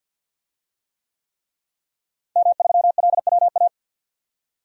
Morse code, heard as a keyed tone, about 51 wpm, 700 Hz. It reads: M3LWA
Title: DXCC-100 Call Signs - 5 Characters - Repeated 3x - 50wpm